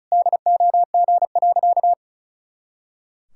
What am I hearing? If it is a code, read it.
Morse code, heard as a keyed tone, about 36 wpm, 700 Hz. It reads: DOG.
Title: Pangram Challenge v2 - 35wpm